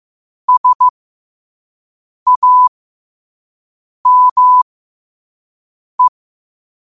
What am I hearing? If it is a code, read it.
Morse code, heard as a keyed tone, about 15 wpm, spaced as for 5 wpm, 1000 Hz. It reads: SAME
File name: qotdfiles / 5wpm